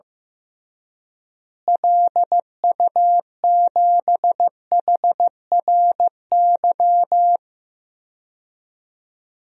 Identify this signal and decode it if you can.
Morse code, heard as a keyed tone, about 15 wpm, 700 Hz. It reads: LU7HRY
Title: DXCC-100 Call Signs - 6 Characters - Repeated 3x - 15wpm